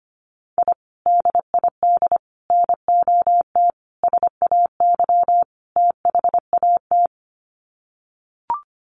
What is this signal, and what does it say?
Morse code, heard as a keyed tone, about 25 wpm, 700 Hz. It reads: I DID NOT SAY THAT
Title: Sets of 5 Words Encore2 25wpm